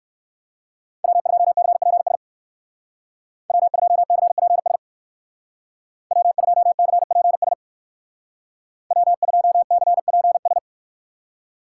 Morse code, heard as a keyed tone, about 54 wpm, 700 Hz. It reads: W2CPS W2CPS W2CPS W2CPS
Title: Speed Racing - USA 1x3 Call Signs 45wpm